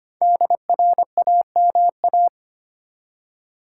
Morse code, heard as a keyed tone, about 25 wpm, 700 Hz. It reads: DRAMA
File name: 5 Letter Words - First 3 Letters Given - 25wpm